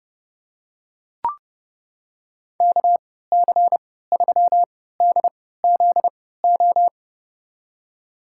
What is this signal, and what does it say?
Morse code, heard as a keyed tone, about 30 wpm, spaced as for 17 wpm, 700 Hz. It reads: KC3DZO
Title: USA Call Signs - 3x Character Spacing - 30wpm